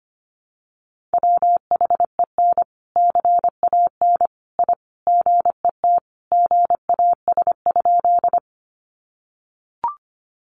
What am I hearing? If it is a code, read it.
Morse code, heard as a keyed tone, about 25 wpm, 700 Hz. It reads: WHEN CAN I GET GAS?
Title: Sentences from Top 1100 Words 25wpm